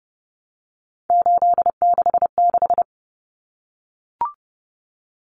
Morse code, heard as a keyed tone, about 30 wpm, 700 Hz. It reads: 866
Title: Common Cut Numbers - 3-Digit 30wpm